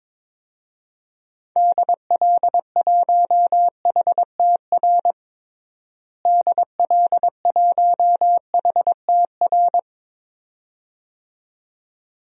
Morse code, heard as a keyed tone, about 22 wpm, 700 Hz. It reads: DL1HTR DL1HTR
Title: DXCC-100 Call Signs - 6 Characters - Repeated 2x - 22wpm